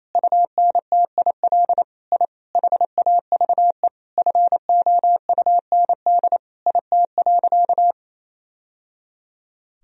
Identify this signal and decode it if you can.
Morse code, heard as a keyed tone, about 28 wpm, 700 Hz. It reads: UNTIL I HAVE FOUND IT.
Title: Sentences from Top 1000 Words - Review Part 1 - 28wpm